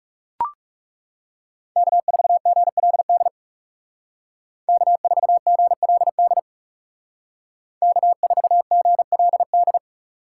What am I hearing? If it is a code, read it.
Morse code, heard as a keyed tone, about 35 wpm, 700 Hz. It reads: K4GLD K4GLD K4GLD
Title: Speed Racing - USA 1x3 Call Signs 30wpm